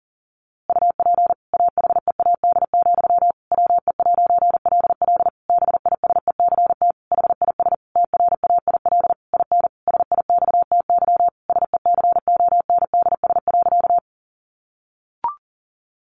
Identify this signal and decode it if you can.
Morse code, heard as a keyed tone, about 40 wpm, 700 Hz. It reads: UP AHEAD, WE'LL BISECT HIS TRAIL IN SIXTY SECONDS.
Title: A Sound Of Thunder 40wpm